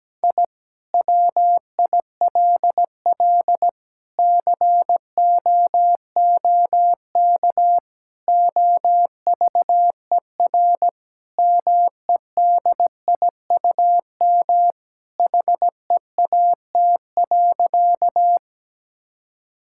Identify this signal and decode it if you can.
Morse code, heard as a keyed tone, about 17 wpm, 700 Hz. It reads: I WILL COOK OVER MEDIUM HEAT.
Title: Sentences from Top 1000 Words - Review Part 6 - 17wpm